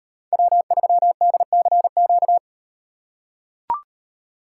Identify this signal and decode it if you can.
Morse code, heard as a keyed tone, about 36 wpm, 700 Hz. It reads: W3DCQ